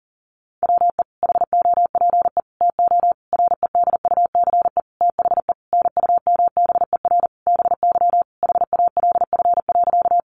Morse code, heard as a keyed tone, about 40 wpm, 700 Hz. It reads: WE HOPE TO REDUCE THE NUMBER BY HALF.
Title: Sentences from Top 1100 Words 40wpm